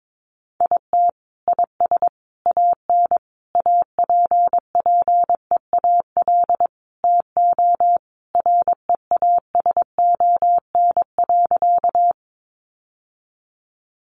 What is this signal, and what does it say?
Morse code, heard as a keyed tone, about 22 wpm, 700 Hz. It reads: IT IS AN APPEAL TO REASON.
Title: Sentences from Top 1300 Words 22wpm